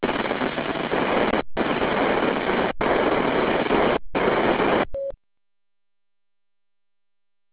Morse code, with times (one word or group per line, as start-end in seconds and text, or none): none